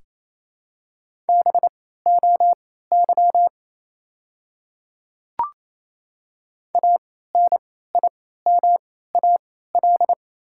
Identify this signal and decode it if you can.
Morse code, heard as a keyed tone, about 28 wpm, spaced as for 16 wpm, 700 Hz. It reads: BOY ANIMAL
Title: 3x Character Spacing - Top 300 Words - 28wpm